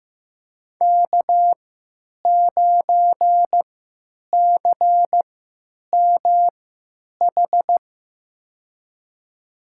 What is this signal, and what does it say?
Morse code, heard as a keyed tone, about 15 wpm, spaced as for 9 wpm, 700 Hz. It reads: K9CMH
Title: USA Call Signs - 5 Characters - 3x Character Spacing - 15wpm